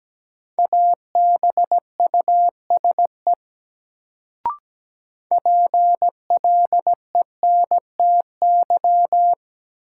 Morse code, heard as a keyed tone, about 17 wpm, 700 Hz. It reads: ABUSE PLENTY
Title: Sentences from Top 1300 Words - New Word Drill - 17wpm